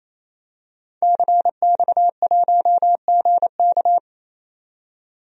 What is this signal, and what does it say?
Morse code, heard as a keyed tone, about 27 wpm, 700 Hz. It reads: CX1GK
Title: DXCC-100 Call Signs - 5 Characters - 28wpm